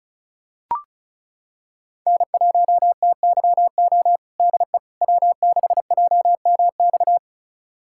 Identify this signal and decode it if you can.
Morse code, heard as a keyed tone, about 36 wpm, 700 Hz. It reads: N1TYO DE W6JMX
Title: USA Callsign DE Callsign - 5 Characters - 35wpm